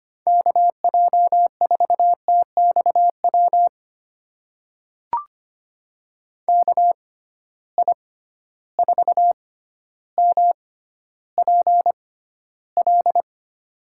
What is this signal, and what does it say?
Morse code, heard as a keyed tone, about 25 wpm, spaced as for 13 wpm, 700 Hz. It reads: KJ4TXW KI4MPL
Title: USA Call Signs - 6 Characters - 6x Character Spacing - 25wpm